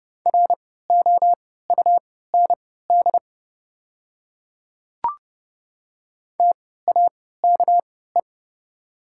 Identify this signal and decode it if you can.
Morse code, heard as a keyed tone, about 30 wpm, spaced as for 17 wpm, 700 Hz. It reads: ROUND TAKE
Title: 3x Character Spacing - Top 200 Words - 30wpm